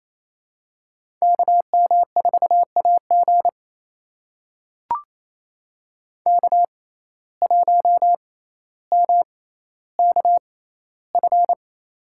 Morse code, heard as a keyed tone, about 27 wpm, spaced as for 14 wpm, 700 Hz. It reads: KM4AG K1MKF
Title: USA Call Signs - 5 Characters - 6x Character Spacing - 28wpm